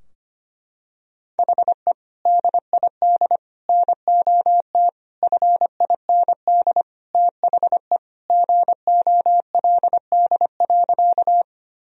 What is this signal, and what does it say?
Morse code, heard as a keyed tone, about 25 wpm, 700 Hz. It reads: HE DID NOT FIND THE GOLD.
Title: Sentences from Top 1000 Words - Review Part 3 - 25wpm